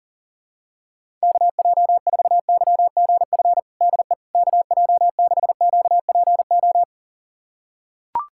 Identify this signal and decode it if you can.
Morse code, heard as a keyed tone, about 40 wpm, 700 Hz. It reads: KJ4YGF DE KJ6QPO
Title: USA Callsign DE Callsign 40wpm